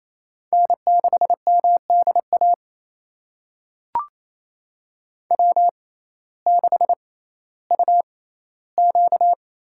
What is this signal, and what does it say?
Morse code, heard as a keyed tone, about 28 wpm, spaced as for 15 wpm, 700 Hz. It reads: N6MDA W6UQ
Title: USA Call Signs - 5 Characters - 6x Character Spacing - 28wpm